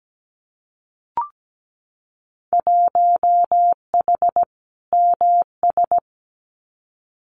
Morse code, heard as a keyed tone, about 17 wpm, 700 Hz. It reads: 1H MS
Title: Winter Field Day Exchanges 17wpm